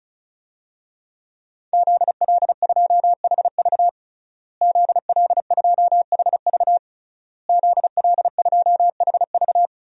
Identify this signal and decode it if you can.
Morse code, heard as a keyed tone, about 35 wpm, 700 Hz. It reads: ZL2HV ZL2HV ZL2HV
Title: DXCC-100 Call Signs - 5 Characters - Repeated 3x - 35wpm